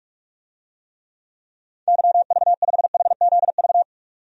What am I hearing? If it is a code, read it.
Morse code, heard as a keyed tone, about 43 wpm, 700 Hz. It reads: YV5HZV
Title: DXCC-100 Call Signs - 6 Characters - 45wpm